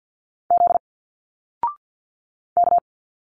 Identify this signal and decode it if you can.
Morse code, heard as a keyed tone, about 49 wpm, 700 Hz. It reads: Z K
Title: Single Letters 50wpm